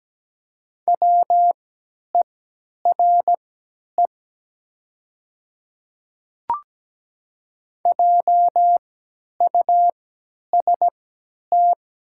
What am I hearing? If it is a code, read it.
Morse code, heard as a keyed tone, about 17 wpm, spaced as for 10 wpm, 700 Hz. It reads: WERE JUST
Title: 3x Character Spacing - Top 200 Words - 17wpm